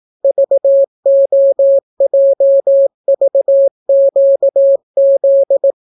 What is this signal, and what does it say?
Morse code, heard as a keyed tone, about 18 wpm, 550 Hz. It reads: VOJVQZ